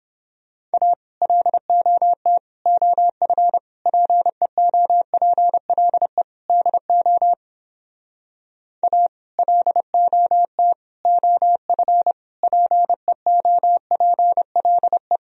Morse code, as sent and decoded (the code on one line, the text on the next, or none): .- / .-.. --- - / --- ..-. / .--. --- .--. .-.. / -.. --- / .- / .-.. --- - / --- ..-. / .--. --- .--. .-.. .
A LOT OF PEOPLE DO A LOT OF PEOPLE